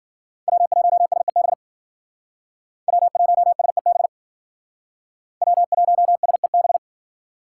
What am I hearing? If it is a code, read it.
Morse code, heard as a keyed tone, about 52 wpm, 700 Hz. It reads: W1SEB W1SEB W1SEB